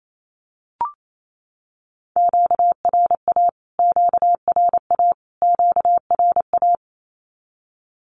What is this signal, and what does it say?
Morse code, heard as a keyed tone, about 28 wpm, 700 Hz. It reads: QRA QRA QRA
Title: Q-Codes Repeated 3x 28wpm